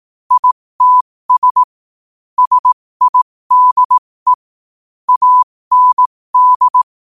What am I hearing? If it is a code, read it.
Morse code, heard as a keyed tone, about 18 wpm, spaced as for 15 wpm, 1000 Hz. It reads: ITS SIDE AND
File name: qotdfiles / 15wpm